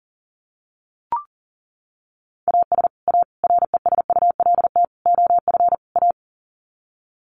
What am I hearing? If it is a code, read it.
Morse code, heard as a keyed tone, about 37 wpm, 700 Hz. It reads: AS A RESULT OF A
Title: Sets of 5 Words Encore 40wpm